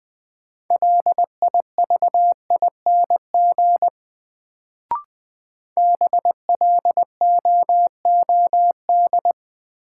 Morse code, heard as a keyed tone, about 20 wpm, 700 Hz. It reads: LIVING BLOOD
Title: Sentences from Top 1100 Words - New Word Drill Alt - 20wpm